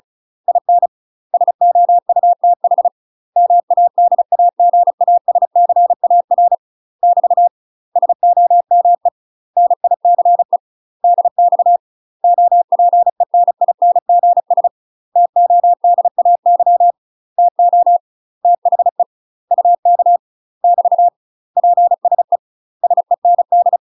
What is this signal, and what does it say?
Morse code, heard as a keyed tone, about 36 wpm, 700 Hz. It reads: IN SOUTH MADAGASCAR <BT> SOME NICE DX OPENINGS TODAY TO THE UK <BT> PSE SEND
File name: UK MASTER QSOs TRAINING / 35WPM / UK MASTER QSOs 0029